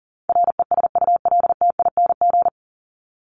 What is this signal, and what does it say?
Morse code, heard as a keyed tone, about 40 wpm, 700 Hz. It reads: RESULTING
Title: ING Suffix - Top 100 Words with Base Words 40wpm